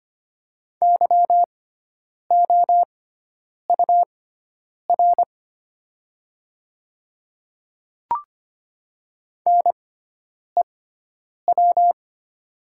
Morse code, heard as a keyed tone, about 25 wpm, spaced as for 9 wpm, 700 Hz. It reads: YOUR NEW
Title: 6x Character Spacing - Top 300 Words - 25wpm